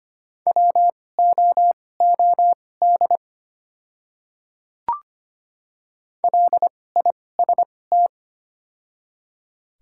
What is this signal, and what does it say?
Morse code, heard as a keyed tone, about 25 wpm, spaced as for 18 wpm, 700 Hz. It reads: WOOD LIST
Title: 2x Character Spacing - Top 500 Words - 25wpm